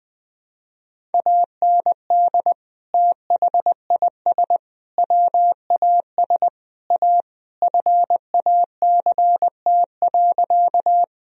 Morse code, heard as a keyed tone, about 20 wpm, 700 Hz. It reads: AND THIS WAS A FACT.
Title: Sentences from Top 1000 Words - Review Part 3 - 20wpm